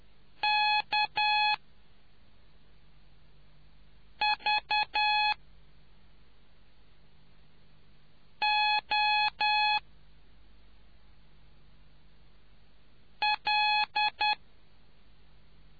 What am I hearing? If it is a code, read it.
Morse code, heard as a keyed tone, about 10 wpm, spaced as for 3 wpm, 840 Hz. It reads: KVOL